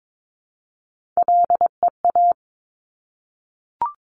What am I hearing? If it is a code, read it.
Morse code, heard as a keyed tone, about 22 wpm, 700 Hz. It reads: LEA
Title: Top Three Letter Combinations in Words 22wpm